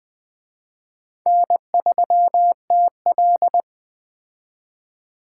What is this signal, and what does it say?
Morse code, heard as a keyed tone, about 21 wpm, 700 Hz. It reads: N3TL
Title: Speed Racing - USA 1x2 Call Signs 17wpm